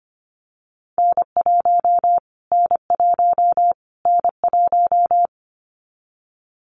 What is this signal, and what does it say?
Morse code, heard as a keyed tone, about 25 wpm, 700 Hz. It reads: N1 N1 N1
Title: Callsign Prefixes for Top 25 DXCC - Repeated 3x - 25wpm